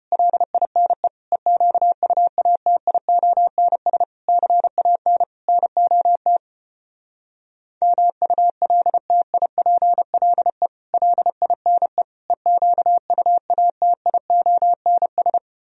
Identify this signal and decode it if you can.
Morse code, heard as a keyed tone, about 33 wpm, 700 Hz. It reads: LINE EQUATIONS CAN NOT MULTIPLE LINE EQUATIONS